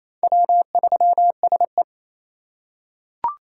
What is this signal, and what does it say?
Morse code, heard as a keyed tone, about 28 wpm, 700 Hz. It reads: W3SE
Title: USA Call Signs - 4x Character Spacing - 28wpm